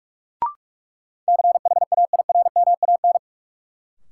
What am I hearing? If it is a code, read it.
Morse code, heard as a keyed tone, about 46 wpm, 700 Hz. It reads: CHAIRMAN
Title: Sets of 3 Words - New Word Drill - 45wpm